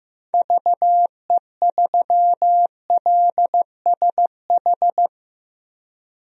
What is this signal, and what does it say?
Morse code, heard as a keyed tone, about 15 wpm, 700 Hz. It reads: VE3LSH
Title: DXCC-100 Call Signs - 6 Characters - Repeated 3x - 15wpm